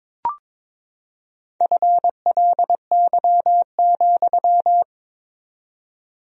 Morse code, heard as a keyed tone, about 21 wpm, 700 Hz. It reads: FLY,
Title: 4x Word-Spacing - Sentences from Top 500 Words - 22wpm